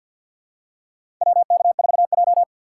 Morse code, heard as a keyed tone, about 52 wpm, 700 Hz. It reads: WK4J
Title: USA Call Signs - 2x Character Spacing - 50wpm